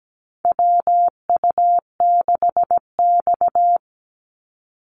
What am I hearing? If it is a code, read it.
Morse code, heard as a keyed tone, about 17 wpm, 700 Hz. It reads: WU6X
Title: Call Signs Repeated 2x - 17wpm